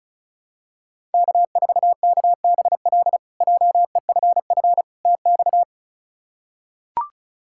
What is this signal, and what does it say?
Morse code, heard as a keyed tone, about 36 wpm, 700 Hz. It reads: K4KBL JEFF TX
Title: K1USN SST QSO Practice - 35wpm